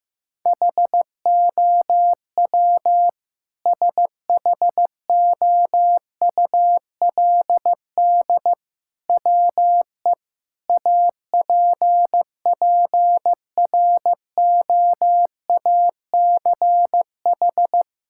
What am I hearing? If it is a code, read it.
Morse code, heard as a keyed tone, about 15 wpm, 700 Hz. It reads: HOW SHOULD WE APPROACH
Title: Sentences from Top 1000 Words - Review Part 6 - 15wpm